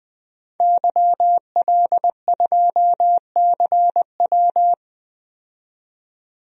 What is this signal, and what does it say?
Morse code, heard as a keyed tone, about 21 wpm, 700 Hz. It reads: YL2CW
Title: DXCC-100 Call Signs - 5 Characters - 20wpm